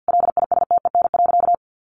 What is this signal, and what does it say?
Morse code, heard as a keyed tone, about 50 wpm, 700 Hz. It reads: LISTEN.
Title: Sentences from Top 1300 Words 50wpm